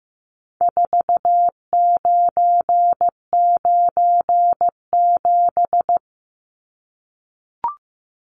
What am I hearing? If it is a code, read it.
Morse code, heard as a keyed tone, about 15 wpm, 700 Hz. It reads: 4997
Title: Numbers - 4-Digit 15wpm